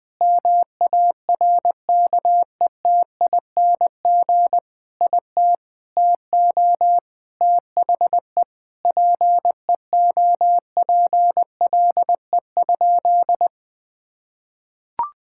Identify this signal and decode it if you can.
Morse code, heard as a keyed tone, about 21 wpm, 700 Hz. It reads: MARKETING IT TO THE PEOPLE?
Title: Sentences from Top 1000 Words - Review Part 5 - 20wpm